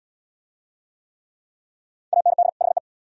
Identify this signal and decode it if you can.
Morse code, heard as a keyed tone, about 79 wpm, 700 Hz. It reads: AND HE